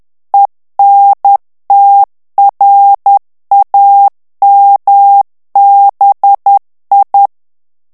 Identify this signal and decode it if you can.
Morse code, heard as a keyed tone, about 10 wpm, 800 Hz. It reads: ENTRAMBI
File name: ITALIEN texte clair 2 vitesse 10 mots minute